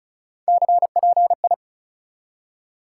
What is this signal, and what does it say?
Morse code, heard as a keyed tone, about 36 wpm, 700 Hz. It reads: CPI
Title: Top 180 QSO Elements 35wpm